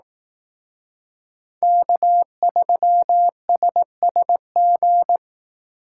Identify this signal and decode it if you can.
Morse code, heard as a keyed tone, about 18 wpm, 700 Hz. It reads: K3SSG
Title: Speed Racing - USA 1x3 Call Signs 15wpm